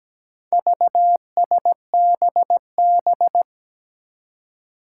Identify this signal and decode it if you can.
Morse code, heard as a keyed tone, about 17 wpm, 700 Hz. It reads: VSBB
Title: Four Challenging Letter-Number Combinations 17wpm